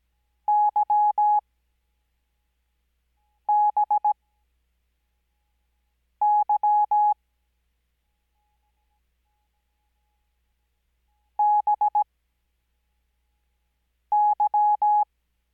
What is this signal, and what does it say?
Morse code, heard as a keyed tone, about 17 wpm, spaced as for 4 wpm, 840 Hz. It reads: YBY BY
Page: Farnsworth Spacing: 15WPM Character speed Spaced at 5WPM.